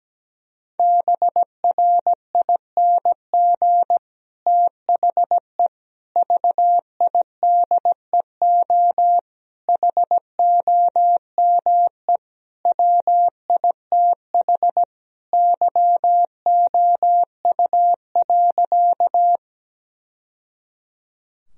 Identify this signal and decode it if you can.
Morse code, heard as a keyed tone, about 17 wpm, 700 Hz. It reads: BRING THE VIDEO HOME WITH YOU.
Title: Sentences from Top 1000 Words - Review Part 4 - 17wpm